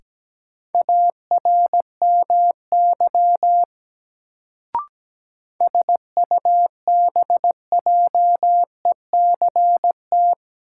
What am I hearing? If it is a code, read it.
Morse code, heard as a keyed tone, about 17 wpm, 700 Hz. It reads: ARMY SUBJECT
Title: Sentences from Top 800 Words - New Word Drill - 17wpm